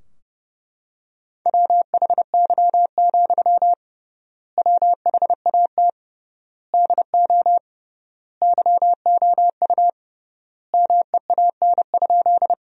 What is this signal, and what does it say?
Morse code, heard as a keyed tone, about 32 wpm, 700 Hz. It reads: WHY, WHAT DO YOU MEAN?
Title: 3x Word-Spacing - Sentences from Top 200 Words - 30wpm